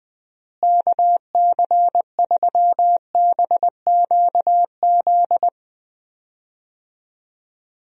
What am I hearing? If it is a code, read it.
Morse code, heard as a keyed tone, about 20 wpm, 700 Hz. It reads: KC3BQZ
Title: USA Call Signs - 6 Characters - Repeated 2x - 20wpm